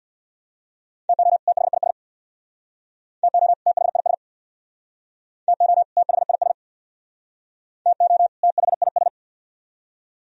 Morse code, heard as a keyed tone, about 68 wpm, 690 Hz. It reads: TO THIS TO THIS TO THIS TO THIS